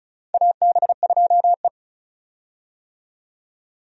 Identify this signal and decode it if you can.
Morse code, heard as a keyed tone, about 36 wpm, 700 Hz. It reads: AB2E
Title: USA Call Signs - 4 Characters - 35wpm